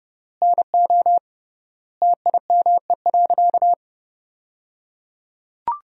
Morse code, heard as a keyed tone, about 30 wpm, 700 Hz. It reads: NO TIME.
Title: 3x Word-Spacing - Sentences from Top 100 Words - 30wpm